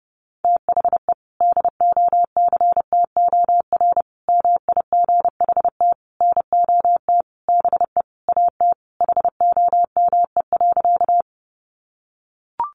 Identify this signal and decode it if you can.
Morse code, heard as a keyed tone, about 28 wpm, 700 Hz. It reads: THE DOCTOR MIGHT NOT BE AT HOME.
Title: Sentences from Top 1500 Words 30wpm